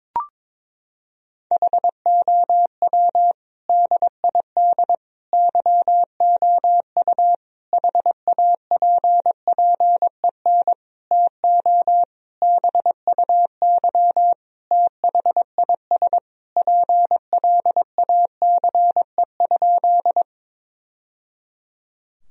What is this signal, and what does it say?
Morse code, heard as a keyed tone, about 22 wpm, 700 Hz. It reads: HOW DID YOU HAPPEN TO BUY THIS PLACE?
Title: Sentences from Top 1000 Words - Review Part 4 - 22wpm